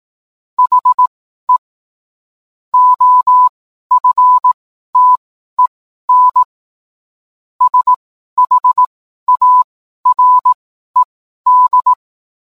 Morse code, heard as a keyed tone, about 18 wpm, spaced as for 12 wpm, 1000 Hz. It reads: HE OFTEN SHARED